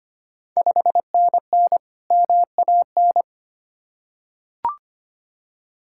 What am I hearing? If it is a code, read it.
Morse code, heard as a keyed tone, about 25 wpm, 700 Hz. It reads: 5NN MAN
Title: Ontario QSO Party Exchange 25wpm